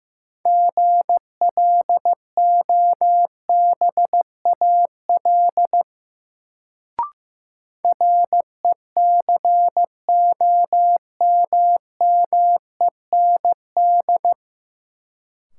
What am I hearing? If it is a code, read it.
Morse code, heard as a keyed tone, about 16 wpm, 700 Hz. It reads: GLOBAL RECOMMEND
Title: Sentences from Top 1200 Words - New Word Drill - 15wpm